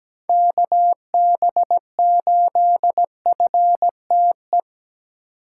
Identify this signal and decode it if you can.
Morse code, heard as a keyed tone, about 17 wpm, 700 Hz. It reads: KB8FTE
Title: USA Call Signs - 6 Characters - 4x Character Spacing - 17wpm